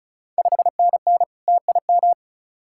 Morse code, heard as a keyed tone, about 36 wpm, 690 Hz. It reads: 5NN TIM
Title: Ontario QSO Party Exchange 35wpm